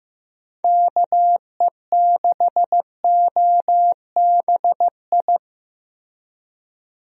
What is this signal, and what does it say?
Morse code, heard as a keyed tone, about 16 wpm, 700 Hz. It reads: KE6OBI
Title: USA Call Signs - 6 Characters - Repeated 3x - 15wpm